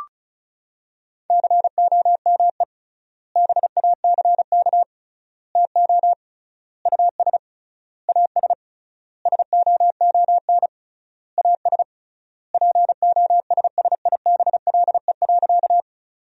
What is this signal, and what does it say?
Morse code, heard as a keyed tone, about 34 wpm, 700 Hz. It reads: COME BACK TO US AS SOON AS POSSIBLE.
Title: 3x Word-Spacing - Sentences from Top 500 Words - 35wpm